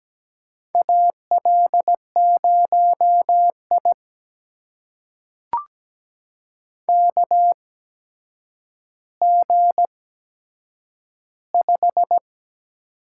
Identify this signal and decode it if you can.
Morse code, heard as a keyed tone, about 17 wpm, spaced as for 8 wpm, 700 Hz. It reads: AL0I KG5
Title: USA Call Signs - 8x Character Spacing - 17wpm